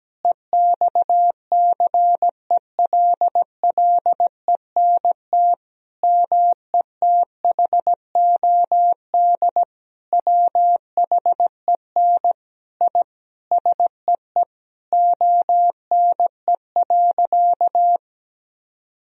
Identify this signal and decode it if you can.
Morse code, heard as a keyed tone, about 17 wpm, 700 Hz. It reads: EXCELLENT METHOD WHEN I SEE ONE.
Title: Sentences from Top 1000 Words - Review Part 3 - 17wpm